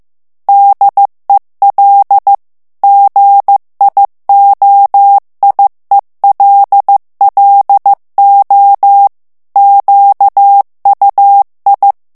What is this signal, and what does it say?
Morse code, heard as a keyed tone, about 15 wpm, 800 Hz. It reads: DEL GIOIELLO QUI
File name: ITALIEN texte clair 1 vitesse 14 mots minute